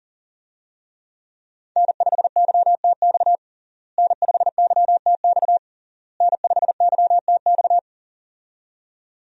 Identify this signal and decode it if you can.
Morse code, heard as a keyed tone, about 40 wpm, 700 Hz. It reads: N5YTX N5YTX N5YTX
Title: USA Call Signs - 5 Characters - Repeated 3x - 40wpm